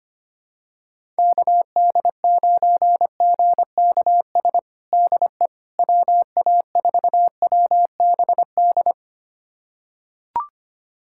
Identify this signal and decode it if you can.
Morse code, heard as a keyed tone, about 25 wpm, 700 Hz. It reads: KD9GKS DE WA4WBD
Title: USA Callsign DE Callsign - 6 Characters - 25wpm